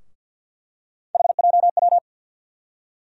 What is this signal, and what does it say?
Morse code, heard as a keyed tone, about 52 wpm, 700 Hz. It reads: HJW
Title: USA Call Sign Postfixes - 3 Letters 50wpm